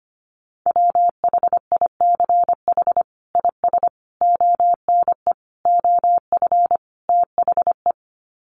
Morse code, heard as a keyed tone, about 25 wpm, 700 Hz. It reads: WHICH IS ONE OF THE
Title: Sets of 5 Words Encore 25wpm